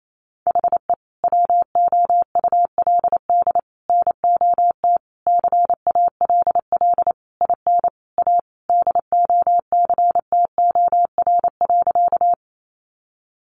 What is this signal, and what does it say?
Morse code, heard as a keyed tone, about 28 wpm, 700 Hz. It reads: HE WOULD NOT CALL IN A DOCTOR.
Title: Sentences from Top 1500 Words 28wpm